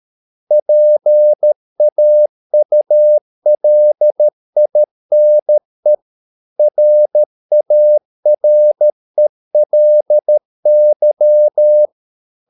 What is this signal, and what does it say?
Morse code, heard as a keyed tone, about 13 wpm, 600 Hz. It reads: PAULINE RARELY